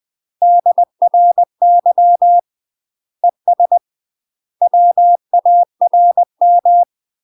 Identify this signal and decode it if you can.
Morse code, heard as a keyed tone, about 20 wpm, 700 Hz. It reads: DRY ES WARM
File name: UK MASTER QSOs TRAINING / 20WPM / UK MASTER QSOs 0059